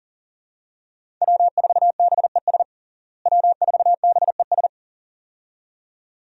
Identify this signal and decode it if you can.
Morse code, heard as a keyed tone, about 40 wpm, 700 Hz. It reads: W4BES W4BES
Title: USA Call Signs - 5 Characters - Repeated 2x - 40wpm